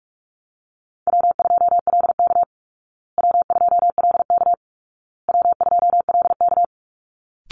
Morse code, heard as a keyed tone, about 44 wpm, 700 Hz. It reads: W2LK W2LK W2LK
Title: Call Signs Repeated 2x - 45wpm